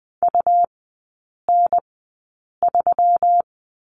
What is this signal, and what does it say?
Morse code, heard as a keyed tone, about 20 wpm, spaced as for 8 wpm, 700 Hz. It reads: UN3
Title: 152 Lesson 25e - Letter p - Three Characters 20wpm